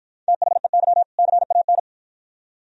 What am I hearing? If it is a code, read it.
Morse code, heard as a keyed tone, about 50 wpm, 700 Hz. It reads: THEY CAN